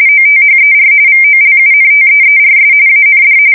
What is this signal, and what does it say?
RTTY, about 45.45 baud. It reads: OTHER VOLUNTEERS AND
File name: fsk170 / battleshipnj